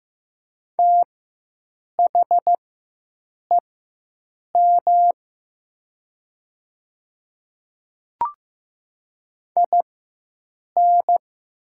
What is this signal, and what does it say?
Morse code, heard as a keyed tone, about 15 wpm, spaced as for 7 wpm, 700 Hz. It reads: THEM IN
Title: 4x Character Spacing - Top 100 Words - 15wpm